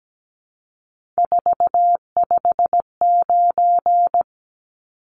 Morse code, heard as a keyed tone, about 17 wpm, 700 Hz. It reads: 459
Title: Numbers - 3-Digit 17wpm